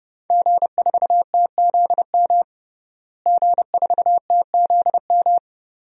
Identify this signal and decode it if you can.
Morse code, heard as a keyed tone, about 30 wpm, 700 Hz. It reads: G4TZM G4TZM
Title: DXCC-100 Call Signs - 5 Characters - Repeated 2x - 30wpm